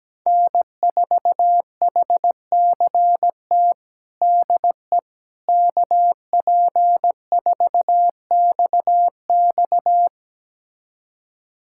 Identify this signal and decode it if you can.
Morse code, heard as a keyed tone, about 17 wpm, 700 Hz. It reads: N4HCT DE KP4XX
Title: USA Callsign DE Callsign - 5 Characters - 17wpm